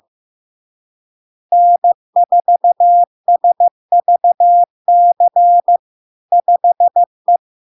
Morse code, heard as a keyed tone, about 15 wpm, 700 Hz. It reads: N4SVC 5E
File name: LARK-FD-2022-Log-in-CW-15wpm-700-Hz